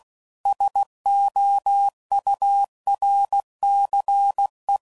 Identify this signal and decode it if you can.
Morse code, heard as a keyed tone, about 16 wpm, 800 Hz. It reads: SOURCE
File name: CW Code Groups @15WPM / Technical Subjects / XC